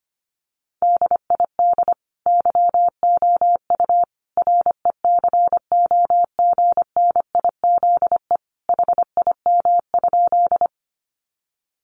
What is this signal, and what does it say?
Morse code, heard as a keyed tone, about 25 wpm, 700 Hz. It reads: DID YOU RECOGNIZE HIM?
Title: Sentences from Top 1400 Words 25wpm